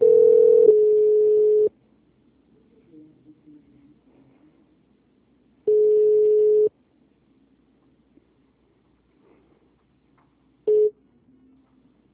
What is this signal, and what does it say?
Ringback tone (Europe, 425 Hz).
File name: out- -2004- - - .388